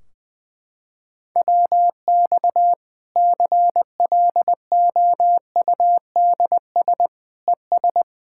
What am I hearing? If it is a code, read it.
Morse code, heard as a keyed tone, about 20 wpm, 700 Hz. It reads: WX CLOUDS ES
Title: Practice QSO - Exchange 2 - Info - 20wpm